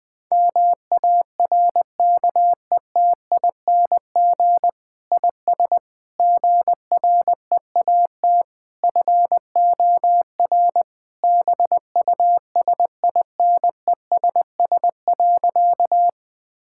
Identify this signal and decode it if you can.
Morse code, heard as a keyed tone, about 20 wpm, 700 Hz. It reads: MARKETING IS GREAT FOR BUSINESS.
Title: Sentences from Top 1000 Words - Review Part 5 - 20wpm